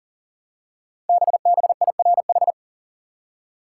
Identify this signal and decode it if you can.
Morse code, heard as a keyed tone, about 39 wpm, 700 Hz. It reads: BBIRH
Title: Code Groups - 5 Letters - 40wpm